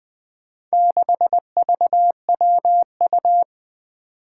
Morse code, heard as a keyed tone, about 20 wpm, 700 Hz. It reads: 6VWU
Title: Four Challenging Letter-Number Combinations 20wpm